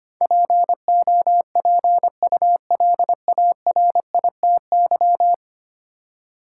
Morse code, heard as a keyed tone, about 25 wpm, 700 Hz. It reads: POPULARITY
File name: Ten Letter Words - 25wpm